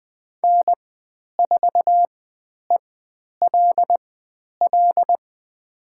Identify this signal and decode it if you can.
Morse code, heard as a keyed tone, about 20 wpm, spaced as for 10 wpm, 700 Hz. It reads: N4ELL
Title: 102 Lesson 17g - Letter u - Call Signs 20wpm